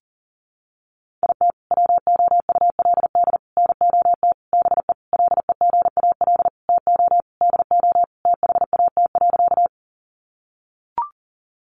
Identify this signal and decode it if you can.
Morse code, heard as a keyed tone, about 40 wpm, 700 Hz. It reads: IT WOULD NOT BE LEGAL TO DO THAT.
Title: Sentences from Top 1200 Words 40wpm